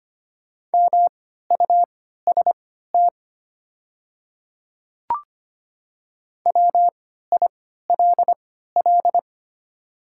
Morse code, heard as a keyed tone, about 25 wpm, spaced as for 14 wpm, 700 Hz. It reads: MUST WILL
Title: 3x Character Spacing - Top 200 Words - 25wpm